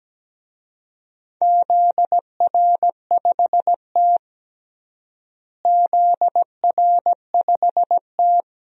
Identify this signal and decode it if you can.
Morse code, heard as a keyed tone, about 17 wpm, 700 Hz. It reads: ZR5T ZR5T
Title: DXCC-100 Call Signs - 4 Characters - Repeated 2x - 17wpm